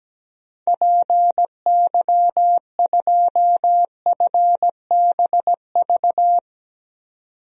Morse code, heard as a keyed tone, about 17 wpm, 700 Hz. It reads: PY2FBV
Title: DXCC-100 Call Signs - 6 Characters - 17wpm